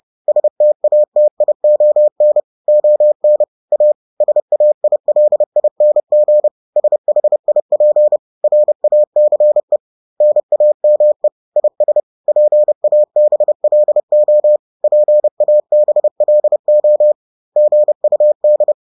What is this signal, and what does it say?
Morse code, heard as a keyed tone, about 30 wpm, 600 Hz. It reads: STATION ON A SAILING SHIP RACE NAME IS PABLO PABLO GUD